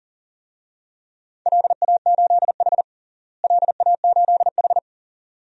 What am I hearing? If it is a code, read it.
Morse code, heard as a keyed tone, about 40 wpm, 700 Hz. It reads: LA8H LA8H
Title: DXCC-100 Call Signs - 4 Characters - Repeated 2x - 40wpm